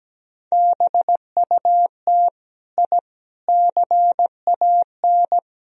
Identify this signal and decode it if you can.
Morse code, heard as a keyed tone, about 18 wpm, 700 Hz. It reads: BUT I CAN
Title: Speed-Racing - Sets of 4 Words 17wpm